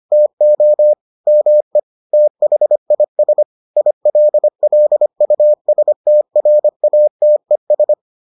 Morse code, heard as a keyed tone, about 25 wpm, 600 Hz. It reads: TO ME THIS ILLUSTRATES